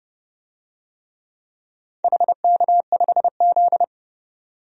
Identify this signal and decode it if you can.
Morse code, heard as a keyed tone, about 30 wpm, 700 Hz. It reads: HK5Z
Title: DXCC-100 Call Signs - 4 Characters - 30wpm